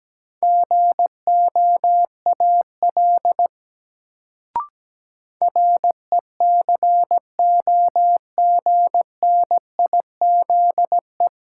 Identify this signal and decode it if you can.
Morse code, heard as a keyed tone, about 17 wpm, 700 Hz. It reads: GOAL RECOGNIZE
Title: Sentences from Top 1400 Words - New Word Drill Alt - 17wpm